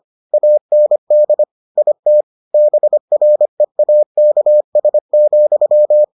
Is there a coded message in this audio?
Yes: Morse code from 0.3 to 6.0 s, heard as a keyed tone, about 25 wpm, 600 Hz. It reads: AND IT BREAKS,